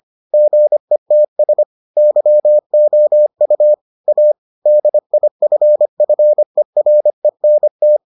Morse code, heard as a keyed tone, about 25 wpm, 600 Hz. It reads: GETS YOU A DIFFERENT